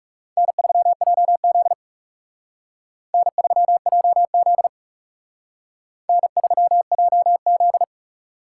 Morse code, heard as a keyed tone, about 38 wpm, 700 Hz. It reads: N3JZ N3JZ N3JZ